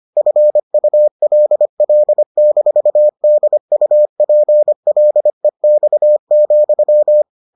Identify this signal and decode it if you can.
Morse code, heard as a keyed tone, about 25 wpm, 600 Hz. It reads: FULL-DUPLEX,